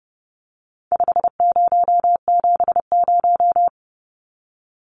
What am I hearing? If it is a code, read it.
Morse code, heard as a keyed tone, about 30 wpm, 700 Hz. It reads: 5070
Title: Numbers - 4-Digit 30wpm